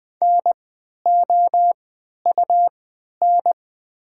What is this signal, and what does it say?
Morse code, heard as a keyed tone, about 20 wpm, spaced as for 11 wpm, 700 Hz. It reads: NOUN
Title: 3x Character Spacing - Top 500 Words - 20wpm